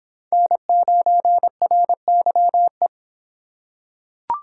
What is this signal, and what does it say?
Morse code, heard as a keyed tone, about 23 wpm, 700 Hz. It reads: N9RYE